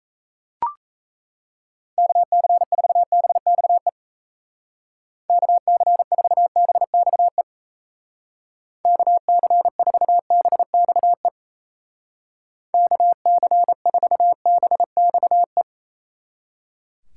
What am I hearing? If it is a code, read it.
Morse code, heard as a keyed tone, about 36 wpm, 700 Hz. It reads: KC4BXE KC4BXE KC4BXE KC4BXE